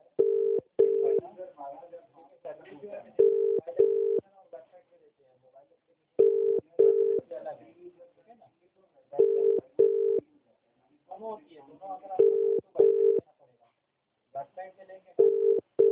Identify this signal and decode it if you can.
Ringback tone (UK), 400 + 450 Hz, with speech.